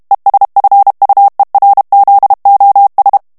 Morse code, heard as a keyed tone, about 32 wpm, 800 Hz. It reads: ESFUERZOS
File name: morse mp3 / ESPAGNOL TEXTE EN CLAIR 1 VITESSE 30 MOTS MINUTES